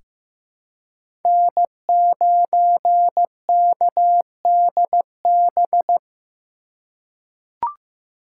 Morse code, heard as a keyed tone, about 15 wpm, 700 Hz. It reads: N9KDB
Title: K1USN SST QSO Practice - 15wpm